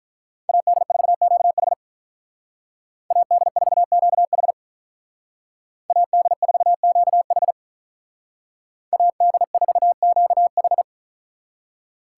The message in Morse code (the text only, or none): AD4QH AD4QH AD4QH AD4QH